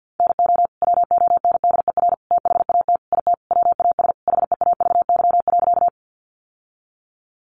Morse code, heard as a keyed tone, about 50 wpm, 700 Hz. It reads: NO WONDER THAT IT WAS HEAVY.
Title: Sentences from Top 1100 Words 50wpm